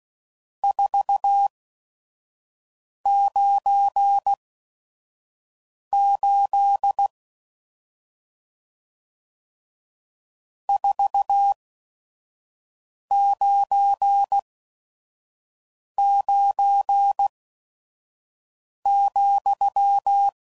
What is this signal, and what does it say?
Morse code, heard as a keyed tone, about 16 wpm, spaced as for 5 wpm, 800 Hz. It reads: 498 499,
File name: CW Code Groups @5WPM / RST's / Various RST No's